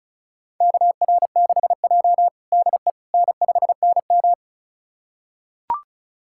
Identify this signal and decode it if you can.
Morse code, heard as a keyed tone, about 36 wpm, 700 Hz. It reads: KR6J DE N5NM
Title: USA Callsign DE Callsign - 4 Characters - 35wpm